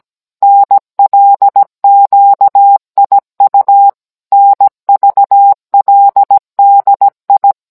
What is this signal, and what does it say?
Morse code, heard as a keyed tone, about 17 wpm, 800 Hz. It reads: NLQIU NVLDI